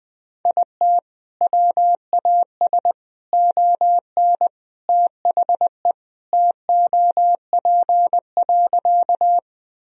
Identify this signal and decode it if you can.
Morse code, heard as a keyed tone, about 20 wpm, 700 Hz. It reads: IT WAS ON THE TOP.
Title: Sentences from Top 1000 Words - Review Part 2 - 20wpm